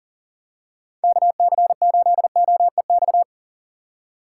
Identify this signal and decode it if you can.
Morse code, heard as a keyed tone, about 38 wpm, 700 Hz. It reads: KC8OEX
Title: USA Call Signs - 6 Characters - 3x Character Spacing - 40wpm